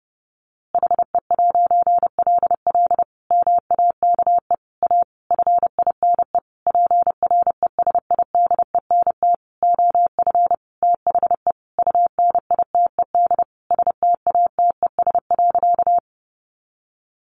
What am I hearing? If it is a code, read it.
Morse code, heard as a keyed tone, about 30 wpm, 700 Hz. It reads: HE'LL MAKE A FINE PRESIDENT OF THE UNITED STATES.
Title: A Sound Of Thunder 30wpm